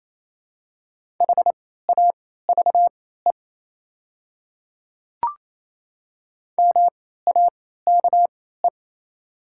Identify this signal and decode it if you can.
Morse code, heard as a keyed tone, about 28 wpm, spaced as for 16 wpm, 700 Hz. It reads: HAVE MAKE
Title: 3x Character Spacing - Top 300 Words - 28wpm